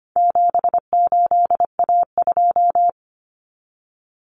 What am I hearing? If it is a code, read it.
Morse code, heard as a keyed tone, about 25 wpm, 700 Hz. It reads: 78A2
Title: Common Cut Numbers - 4-Digit 25wpm